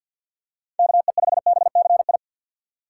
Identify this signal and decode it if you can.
Morse code, heard as a keyed tone, about 52 wpm, 700 Hz. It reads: KE5BCI
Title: USA Call Signs - 6 Characters - 6x Character Spacing - 50wpm